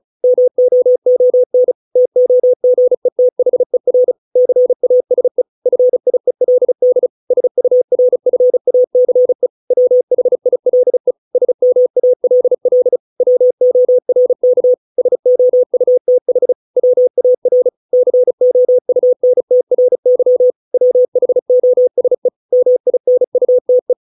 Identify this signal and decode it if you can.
Morse code, heard as a keyed tone, about 34 wpm, 500 Hz. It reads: MOON TOGETHER CASE FIELD SURFACE WHILE SMALL WORK SOUTH WAR COUNTRY WHOSE MINUTE